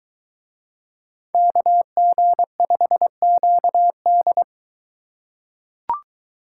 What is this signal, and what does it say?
Morse code, heard as a keyed tone, about 21 wpm, 700 Hz. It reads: KG5QD